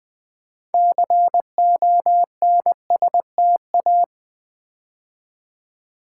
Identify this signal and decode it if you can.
Morse code, heard as a keyed tone, about 20 wpm, 700 Hz. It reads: CONSTA
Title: Build A Word - 3 Letters At A Time - 8-Letter Words - 20wpm